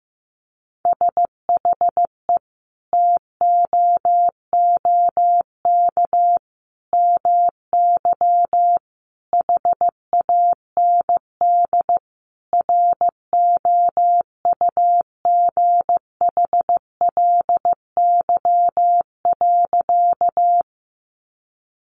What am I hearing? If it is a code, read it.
Morse code, heard as a keyed tone, about 15 wpm, 700 Hz. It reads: SHE TOOK MY HAND ROUGHLY.
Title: Sentences from Top 1500 Words 15wpm